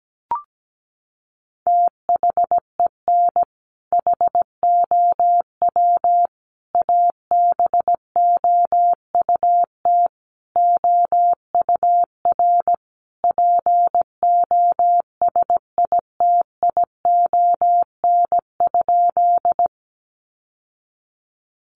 Morse code, heard as a keyed tone, about 17 wpm, 700 Hz. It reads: THEN HOW ABOUT OUR POSITION?
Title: Sentences from Top 900 Words 17wpm